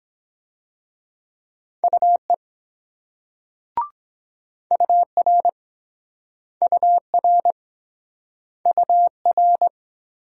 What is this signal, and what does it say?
Morse code, heard as a keyed tone, about 24 wpm, 700 Hz. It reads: UE UR UR UR